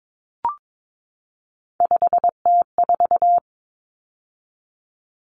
Morse code, heard as a keyed tone, about 22 wpm, 700 Hz. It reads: 5T4
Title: Common Cut Numbers - 3-Digit 22wpm